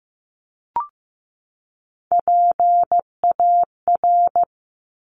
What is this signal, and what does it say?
Morse code, heard as a keyed tone, about 15 wpm, 700 Hz. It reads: PAR
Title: Top Three Letter Combinations in Words 15wpm